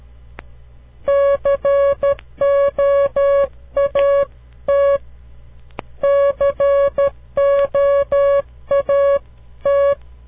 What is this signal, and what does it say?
Morse code, heard as a keyed tone, about 12 wpm, 560 Hz. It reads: COAT COAT
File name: CW Learning Programs / farnsworth / unit 2